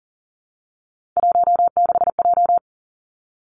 Morse code, heard as a keyed tone, about 43 wpm, 700 Hz. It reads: 16J
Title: Three Challenging Letter-Number Combinations 40wpm